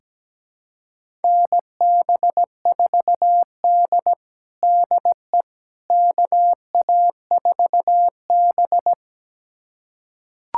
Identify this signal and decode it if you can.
Morse code, heard as a keyed tone, about 17 wpm, 700 Hz. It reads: NB4D DE KA4B
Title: USA Callsign DE Callsign - 4 Characters - 17wpm